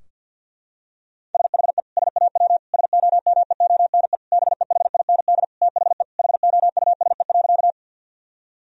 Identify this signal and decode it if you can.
Morse code, heard as a keyed tone, about 50 wpm, 700 Hz. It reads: SHE SAW SOMEONE BEHIND THE HOUSE.
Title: Sentences from Top 1000 Words - Review Part 4 - 50wpm